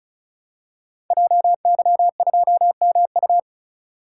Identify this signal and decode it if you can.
Morse code, heard as a keyed tone, about 35 wpm, 700 Hz. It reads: JY2MU
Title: Code Groups - 5 Letter:Numbers - 35wpm